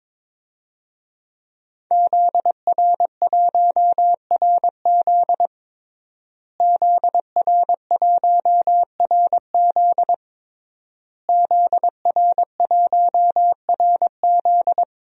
Morse code, heard as a keyed tone, about 22 wpm, 700 Hz. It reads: ZR1RZ ZR1RZ ZR1RZ
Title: DXCC-100 Call Signs - 5 Characters - Repeated 3x - 22wpm